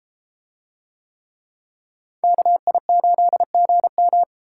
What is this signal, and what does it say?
Morse code, heard as a keyed tone, about 32 wpm, 700 Hz. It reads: KI8GM